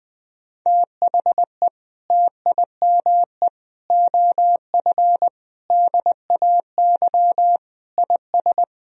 Morse code, heard as a keyed tone, about 21 wpm, 700 Hz. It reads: THE TIME OF DAY IS
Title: Sentences from Top 1000 Words - Review Part 3 - 20wpm